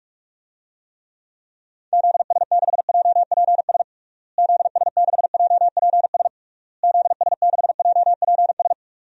Morse code, heard as a keyed tone, about 46 wpm, 700 Hz. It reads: ZS6JPS ZS6JPS ZS6JPS
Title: DXCC-100 Call Signs - 6 Characters - Repeated 3x - 45wpm